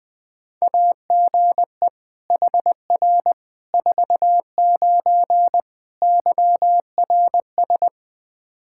Morse code, heard as a keyed tone, about 21 wpm, 700 Hz. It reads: AGE HR 49 YRS
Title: Practice QSO - Exchange 3 - More Info - 20wpm